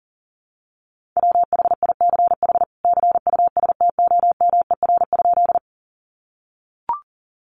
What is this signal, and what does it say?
Morse code, heard as a keyed tone, about 40 wpm, 700 Hz. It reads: WHICH CUSTOMER?
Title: Sentences from Top 1100 Words 40wpm